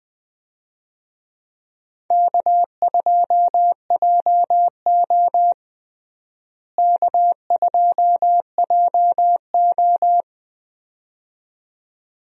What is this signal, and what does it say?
Morse code, heard as a keyed tone, about 20 wpm, 700 Hz. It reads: K2JO K2JO
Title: USA Call Signs - 4 Characters - Repeated 2x - 20wpm